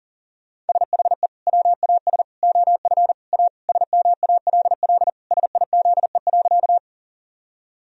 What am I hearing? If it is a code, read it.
Morse code, heard as a keyed tone, about 40 wpm, 700 Hz. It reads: SHE WAS OF A SMALL SIZE.
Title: Sentences from Top 1000 Words - Review Part 2 - 40wpm